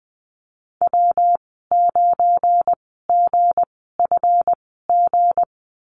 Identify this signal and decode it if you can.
Morse code, heard as a keyed tone, about 20 wpm, spaced as for 15 wpm, 700 Hz. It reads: W9GFG
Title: 179 Lesson 29g - Slash - Call Signs 20wpm